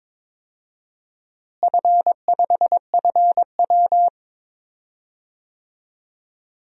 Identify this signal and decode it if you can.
Morse code, heard as a keyed tone, about 21 wpm, 700 Hz. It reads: F5FW
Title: DXCC-100 Call Signs - 4 Characters - Repeated 3x - 22wpm